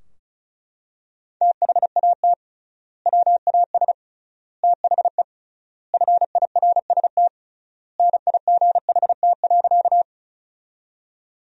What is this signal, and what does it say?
Morse code, heard as a keyed tone, about 36 wpm, 700 Hz. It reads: THAT WAS THE FIRST NIGHT.
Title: 3x Word-Spacing - Sentences from Top 300 Words - 35wpm